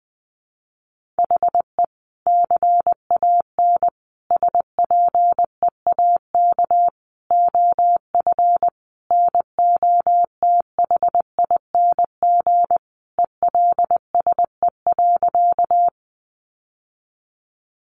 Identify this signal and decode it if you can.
Morse code, heard as a keyed tone, about 20 wpm, 700 Hz. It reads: HE CAN SPEAK OF NOTHING ELSE.
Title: Sentences from Top 1100 Words 20wpm